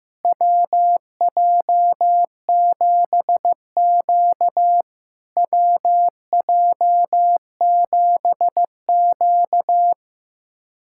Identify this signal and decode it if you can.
Morse code, heard as a keyed tone, about 15 wpm, 700 Hz. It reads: WJ7Q WJ7Q
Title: SOTA QSO Practice 15wpm